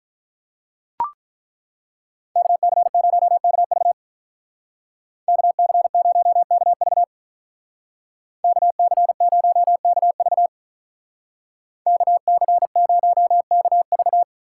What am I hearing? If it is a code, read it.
Morse code, heard as a keyed tone, about 43 wpm, 700 Hz. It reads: KC0KV KC0KV KC0KV KC0KV
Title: Speed Racing - USA 2x2 Call Signs 35wpm